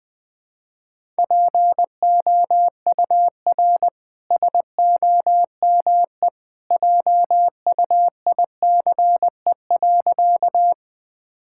Morse code, heard as a keyed tone, about 21 wpm, 700 Hz. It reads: POUR SOME JUICE.
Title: Easy Sentences with Hints - 20wpm